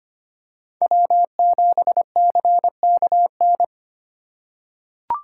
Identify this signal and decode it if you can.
Morse code, heard as a keyed tone, about 25 wpm, 700 Hz. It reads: W7CKN
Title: USA Call Signs - 5 Characters - 2x Character Spacing - 25wpm